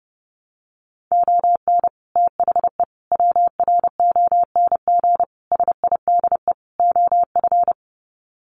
Morse code, heard as a keyed tone, about 30 wpm, 700 Hz. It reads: ON THE WRONG SIDE OF
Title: Sets of 5 Words Encore 30wpm